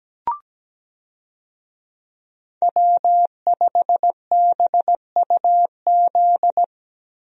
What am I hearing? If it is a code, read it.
Morse code, heard as a keyed tone, about 17 wpm, 700 Hz. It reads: W5BUZ
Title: USA Call Signs - 5 Characters - Repeated 2x - 17wpm